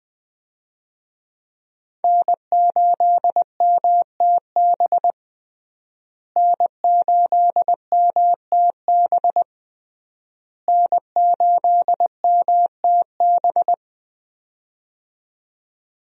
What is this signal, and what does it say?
Morse code, heard as a keyed tone, about 20 wpm, 700 Hz. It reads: N8MTB N8MTB N8MTB
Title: USA Call Signs - 5 Characters - Repeated 3x - 20wpm